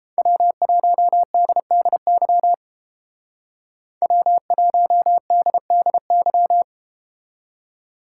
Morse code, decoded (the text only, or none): W1DDY W1DDY